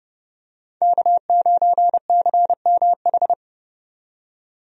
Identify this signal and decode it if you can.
Morse code, heard as a keyed tone, about 31 wpm, 700 Hz. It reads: K9CMH
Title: USA Call Signs - 5 Characters - 30wpm